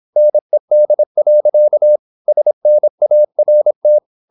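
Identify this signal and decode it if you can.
Morse code, heard as a keyed tone, about 26 wpm, 600 Hz. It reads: NED. SNART
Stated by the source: High-speed (QRQ) Morse at 26 wpm.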